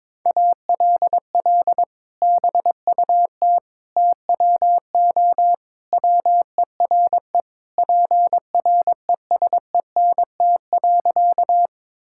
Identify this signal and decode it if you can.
Morse code, heard as a keyed tone, about 22 wpm, 700 Hz. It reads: ALL BUT TWO WERE PRESENT.
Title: Sentences from Top 1000 Words - Review Part 5 - 22wpm